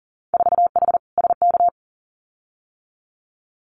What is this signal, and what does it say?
Morse code, heard as a keyed tone, about 40 wpm, 700 Hz. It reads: 4H SK
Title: Winter Field Day Exchanges Encore 40wpm